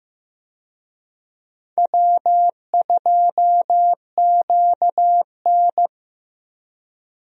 Morse code, heard as a keyed tone, about 15 wpm, 700 Hz. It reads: W2QN
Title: USA Call Signs - 4 Characters - Repeated 3x - 15wpm